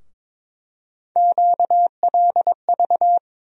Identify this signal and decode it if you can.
Morse code, heard as a keyed tone, about 22 wpm, 700 Hz. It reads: QLV
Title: USA Call Sign Postfixes - 3 Letters 22wpm